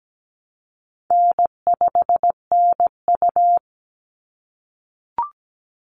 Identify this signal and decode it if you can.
Morse code, heard as a keyed tone, about 17 wpm, 700 Hz. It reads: N5NU
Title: Easy Call Signs - 17wpm